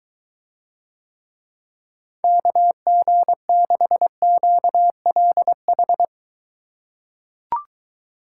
Morse code, heard as a keyed tone, about 22 wpm, 700 Hz. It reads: KG6QLH